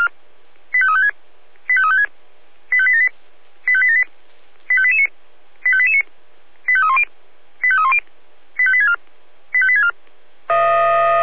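Five-tone selective call (ZVEI), heard as quick runs of five tones, tones of about 70 ms. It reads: 85356 85356 85878 85878 85889 85889 85310 85310 85864 85864
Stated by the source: ZVEI five-tone call received by SDR.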